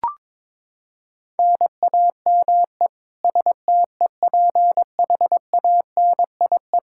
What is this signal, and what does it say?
Morse code, heard as a keyed tone, about 22 wpm, 700 Hz. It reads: NAME STEPHANIE
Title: Practice QSO - Exchange 1 - Beginning - 22wpm